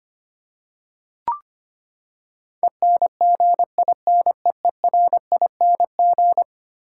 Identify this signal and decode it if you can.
Morse code, heard as a keyed tone, about 25 wpm, 700 Hz. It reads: ENGINEERING
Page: morse-code-ninja-practice-25wpm